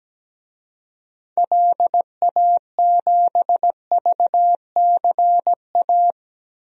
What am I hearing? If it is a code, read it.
Morse code, heard as a keyed tone, about 17 wpm, 700 Hz. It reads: LA7VCA
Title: DXCC-100 Call Signs - 6 Characters - 17wpm